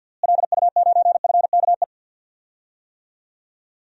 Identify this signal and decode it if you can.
Morse code, heard as a keyed tone, about 51 wpm, 700 Hz. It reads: LU9FKE
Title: DXCC-100 Call Signs - 6 Characters - Repeated 3x - 50wpm